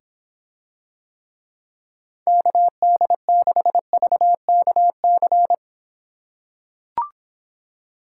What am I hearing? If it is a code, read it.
Morse code, heard as a keyed tone, about 26 wpm, 700 Hz. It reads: KD6VKC